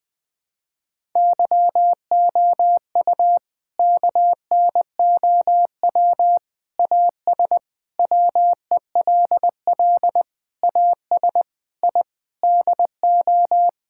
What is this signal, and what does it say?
Morse code, heard as a keyed tone, about 20 wpm, 700 Hz. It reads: YOU KNOW AS WELL AS I DO
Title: Sets of 8 Words - 20wpm